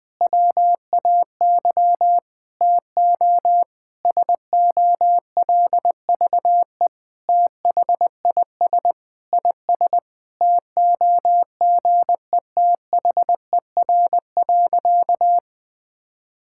Morse code, heard as a keyed tone, about 20 wpm, 700 Hz. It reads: WAY TO SOLVE THIS IS TOGETHER.
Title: Sentences with 4-Word Phrases - 20wpm